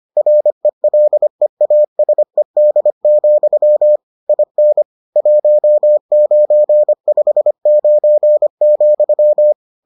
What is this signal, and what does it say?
Morse code, heard as a keyed tone, about 25 wpm, 600 Hz. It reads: RELEASED, IN 1959,